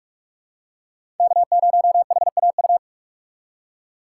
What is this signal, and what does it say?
Morse code, heard as a keyed tone, about 46 wpm, 700 Hz. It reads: K0HAU
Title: USA Call Signs - 5 Characters - 3x Character Spacing - 45wpm